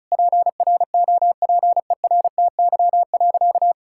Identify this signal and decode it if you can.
Morse code, heard as a keyed tone, about 34 wpm, 700 Hz. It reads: PROPERTY.
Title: Sentences from Top 1600 Words 35wpm